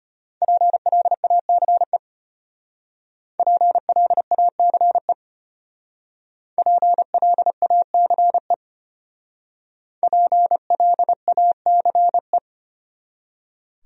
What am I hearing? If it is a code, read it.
Morse code, heard as a keyed tone, about 32 wpm, 700 Hz. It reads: PLACE PLACE PLACE PLACE